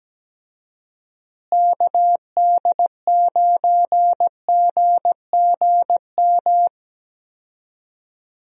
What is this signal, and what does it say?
Morse code, heard as a keyed tone, about 17 wpm, 700 Hz. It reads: KD9GGM
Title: USA Call Signs - 6 Characters - Repeated 3x - 17wpm